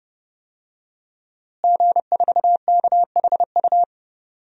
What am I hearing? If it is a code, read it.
Morse code, heard as a keyed tone, about 31 wpm, 700 Hz. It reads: G4KHU
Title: DXCC-100 Call Signs - 5 Characters - Repeated 3x - 30wpm